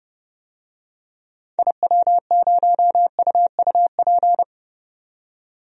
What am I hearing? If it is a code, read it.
Morse code, heard as a keyed tone, about 30 wpm, 700 Hz. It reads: IW0UUP
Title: DXCC-100 Call Signs - 6 Characters - 30wpm